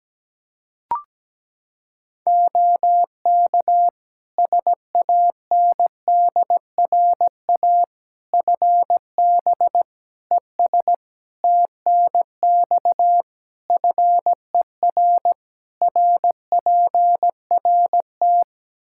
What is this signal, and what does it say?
Morse code, heard as a keyed tone, about 17 wpm, 700 Hz. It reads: OK SANDRA FB ES TNX FER RPRT
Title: Practice QSO - Exchange 2 - Info - 17wpm